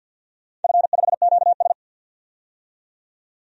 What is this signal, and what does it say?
Morse code, heard as a keyed tone, about 52 wpm, 700 Hz. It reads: F5QS
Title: DXCC-100 Call Signs - 4 Characters - 50wpm